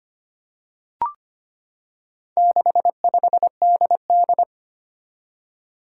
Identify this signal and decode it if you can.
Morse code, heard as a keyed tone, about 25 wpm, 700 Hz. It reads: 65DD
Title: Four Challenging Letter-Number Combinations 25wpm